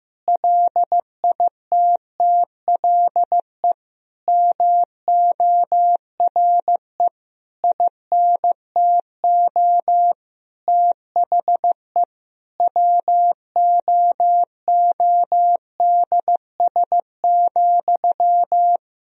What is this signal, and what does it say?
Morse code, heard as a keyed tone, about 15 wpm, 700 Hz. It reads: LITTLE MORE INTO THE WOODS,
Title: Sentences from Top 1000 Words - Review Part 2 - 15wpm